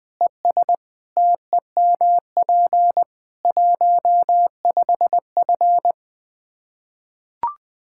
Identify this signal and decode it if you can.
Morse code, heard as a keyed tone, about 20 wpm, 700 Hz. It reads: ES TEMP 15F
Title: Practice QSO - Exchange 2 - Info - 20wpm